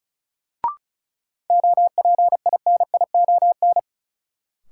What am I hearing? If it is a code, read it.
Morse code, heard as a keyed tone, about 34 wpm, 700 Hz. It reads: OPINION
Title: Sentences from Top 1200 Words - New Word Drill - 35wpm